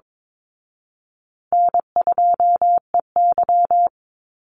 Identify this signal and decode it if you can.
Morse code, heard as a keyed tone, about 22 wpm, 700 Hz. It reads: N2EY
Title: Easy Call Signs Repeated 3x - 22wpm